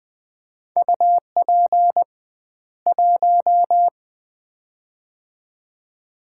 Morse code, heard as a keyed tone, about 20 wpm, 700 Hz. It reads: UP 1
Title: Common QSO Phrases - 2x Word-Spacing - 20wpm